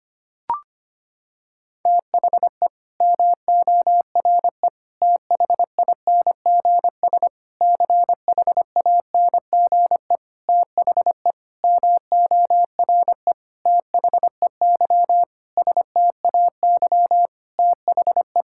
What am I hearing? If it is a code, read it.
Morse code, heard as a keyed tone, about 25 wpm, 700 Hz. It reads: THE MORE THINGS CHANGE THE MORE THEY STAY THE
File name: practice / 25wpm / Sentences with 9-Word Phrases - 25wpm